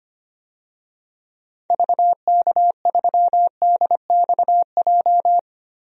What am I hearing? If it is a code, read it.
Morse code, heard as a keyed tone, about 25 wpm, 700 Hz. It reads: VK3DXJ
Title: DXCC-100 Call Signs - 6 Characters - 25wpm